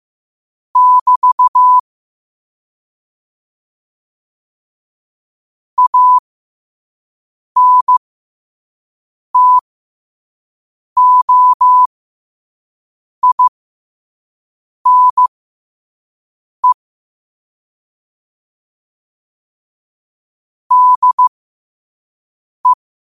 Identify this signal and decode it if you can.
Morse code, heard as a keyed tone, about 15 wpm, spaced as for 5 wpm, 1000 Hz. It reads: <BT> ANTOINE DE